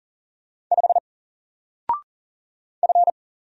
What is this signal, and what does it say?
Morse code, heard as a keyed tone, about 40 wpm, 700 Hz. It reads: 5 F
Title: Single Character (Letter:Number:Punctuation:Prosign) 40wpm